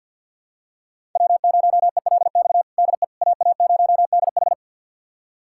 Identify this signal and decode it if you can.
Morse code, heard as a keyed tone, about 51 wpm, 700 Hz. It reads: W0ELX DE AA0DH
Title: USA Callsign DE Callsign - 5 Characters - 50wpm